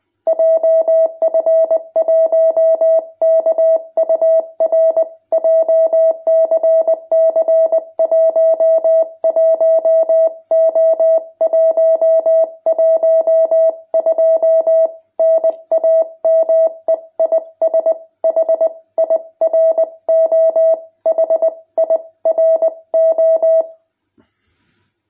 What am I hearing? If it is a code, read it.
Morse code, heard as a keyed tone, about 20 wpm, 660 Hz. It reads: JF1KUR JCC 11O112 NAME IS HIRO HIRO
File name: jh8cbh / 2021cw